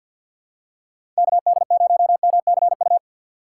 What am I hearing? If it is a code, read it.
Morse code, heard as a keyed tone, about 53 wpm, 700 Hz. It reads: KD0MCU
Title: USA Call Signs - 2x Character Spacing - 50wpm